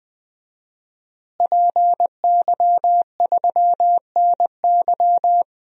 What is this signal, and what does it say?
Morse code, heard as a keyed tone, about 20 wpm, 700 Hz. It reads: PY3NY
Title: DXCC-100 Call Signs - 5 Characters - Repeated 2x - 20wpm